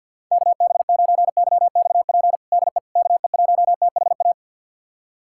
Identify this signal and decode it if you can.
Morse code, heard as a keyed tone, about 52 wpm, 690 Hz. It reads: KB9YXP DE KE1THA
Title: USA Callsign DE Callsign 50wpm